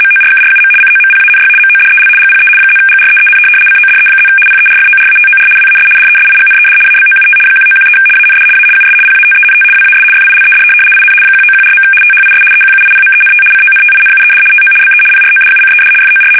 RTTY, about 75 baud. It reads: MORKRUM COMPANY MODEL GPE +GREEN CODE+ PERFORATOR WAS DESIGNED ABOUT 1913 AND A US PATENT WAS FILED IN 1914. THIS EQUIPMENT CONTINUED TO BE PRODUCED FOR
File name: fsk850-75B / teletype-corp